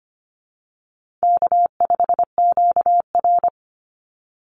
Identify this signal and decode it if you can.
Morse code, heard as a keyed tone, about 25 wpm, 700 Hz. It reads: K5QR
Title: NA SOTA Chaser Callsigns 25wpm